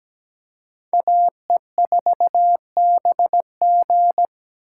Morse code, heard as a keyed tone, about 20 wpm, 700 Hz. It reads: AE4BG
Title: Speed Racing - USA 2x2 Call Signs 17wpm